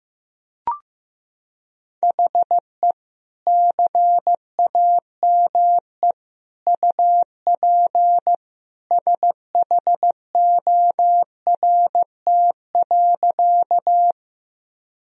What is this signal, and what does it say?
Morse code, heard as a keyed tone, about 15 wpm, 700 Hz. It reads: HE CAME UP SHORT.
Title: Sentences from Top 1000 Words - Review Part 2 - 15wpm